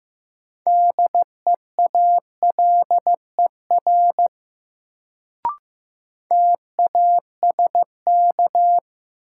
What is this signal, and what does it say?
Morse code, heard as a keyed tone, about 15 wpm, 700 Hz. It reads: DEALER TASK
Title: Sentences from Top 1100 Words - New Word Drill Alt - 15wpm